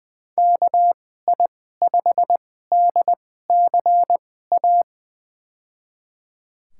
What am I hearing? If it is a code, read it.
Morse code, heard as a keyed tone, about 20 wpm, spaced as for 14 wpm, 700 Hz. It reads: KI5DCA
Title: USA Call Signs - 6 Characters - 2x Character Spacing - 20wpm